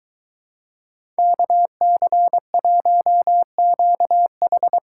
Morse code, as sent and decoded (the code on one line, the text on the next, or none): -.- -.-. .---- --.- ....
KC1QH